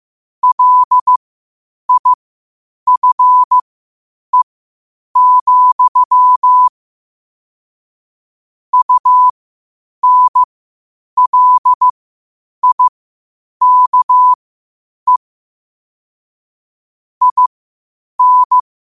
Morse code, heard as a keyed tone, about 15 wpm, spaced as for 8 wpm, 1000 Hz. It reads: LIFE, UNLIKE IN
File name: qotdfiles / 8wpm